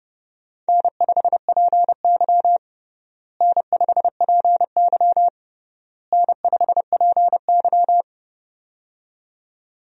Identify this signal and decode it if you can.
Morse code, heard as a keyed tone, about 30 wpm, 700 Hz. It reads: N5PY N5PY N5PY
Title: USA Call Signs - 4 Characters - Repeated 3x - 30wpm